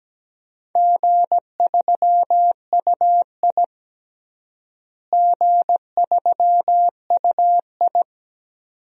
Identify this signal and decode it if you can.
Morse code, heard as a keyed tone, about 17 wpm, 700 Hz. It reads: G3UI G3UI
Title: DXCC-100 Call Signs - 4 Characters - Repeated 2x - 17wpm